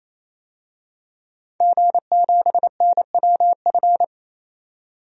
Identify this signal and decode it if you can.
Morse code, heard as a keyed tone, about 28 wpm, 700 Hz. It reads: G7NWF
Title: DXCC-100 Call Signs - 5 Characters - 28wpm